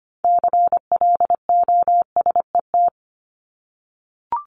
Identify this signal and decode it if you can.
Morse code, heard as a keyed tone, about 25 wpm, 700 Hz. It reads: CLOSET
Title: Top 1751-2000 Words 25wpm